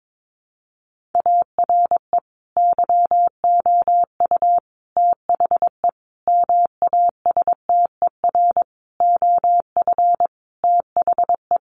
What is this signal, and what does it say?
Morse code, heard as a keyed tone, about 22 wpm, 700 Hz. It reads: ARE YOU THE MASTER OF THE
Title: Sentences from Top 1300 Words 22wpm